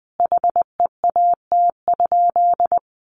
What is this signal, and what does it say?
Morse code, heard as a keyed tone, about 20 wpm, 700 Hz. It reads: HEAT?
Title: Sentences from Top 1400 Words 20wpm